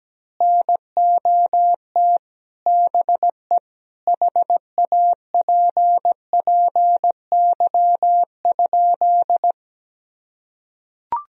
Morse code, heard as a keyed tone, about 17 wpm, 700 Hz. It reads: NOT BE HAPPY?
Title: Sentences from Top 1000 Words - Review Part 2 - 17wpm